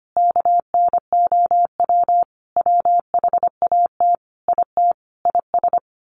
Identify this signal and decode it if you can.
Morse code, heard as a keyed tone, about 25 wpm, 700 Hz. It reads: KNOW WHAT IT IS
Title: Sets of 5 Words Encore 25wpm